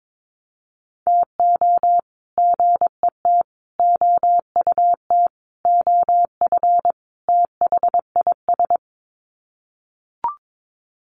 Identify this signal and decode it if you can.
Morse code, heard as a keyed tone, about 22 wpm, 700 Hz. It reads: TO GET OUT OF THIS
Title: Sets of 5 Words Encore2 22wpm